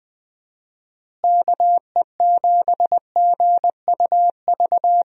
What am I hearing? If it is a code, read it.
Morse code, heard as a keyed tone, about 21 wpm, 700 Hz. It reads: KE7GUV
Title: USA Call Signs - 6 Characters - Repeated 2x - 20wpm